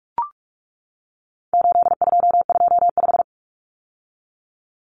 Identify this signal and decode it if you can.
Morse code, heard as a keyed tone, about 44 wpm, 700 Hz. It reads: 8225
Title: Numbers - 4-Digit 45wpm